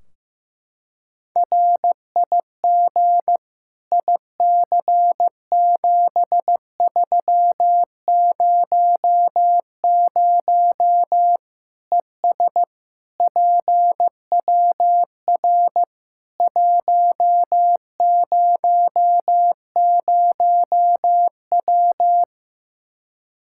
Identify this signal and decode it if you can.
Morse code, heard as a keyed tone, about 15 wpm, 700 Hz. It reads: RIG IC7300 ES PWR 100W
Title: Practice QSO - Exchange 2 - Info - 15wpm